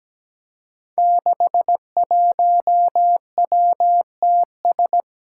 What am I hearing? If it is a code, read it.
Morse code, heard as a keyed tone, about 17 wpm, 700 Hz. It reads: 61WTS
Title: Five Challenging Letter-Number Combinations 17wpm